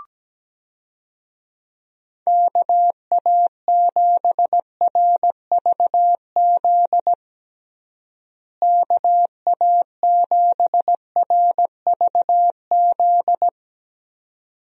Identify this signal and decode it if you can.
Morse code, heard as a keyed tone, about 17 wpm, 700 Hz. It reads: KA7RVZ KA7RVZ
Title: USA Call Signs - 6 Characters - Repeated 2x - 17wpm